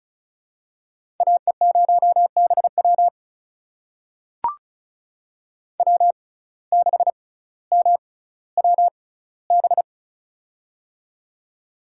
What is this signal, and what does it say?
Morse code, heard as a keyed tone, about 36 wpm, spaced as for 18 wpm, 700 Hz. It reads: AE0BW W6MWB
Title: USA Call Signs - 5 Characters - 6x Character Spacing - 35wpm